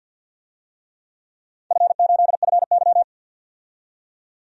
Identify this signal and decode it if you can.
Morse code, heard as a keyed tone, about 53 wpm, 700 Hz. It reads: F8FY
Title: DXCC-100 Call Signs - 4 Characters - 50wpm